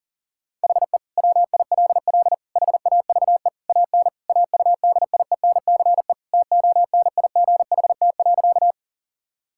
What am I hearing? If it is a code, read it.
Morse code, heard as a keyed tone, about 40 wpm, 700 Hz. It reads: HE WILL HAVE AN AUDIENCE TONIGHT.
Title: Sentences from Top 1000 Words - Review Part 5 - 40wpm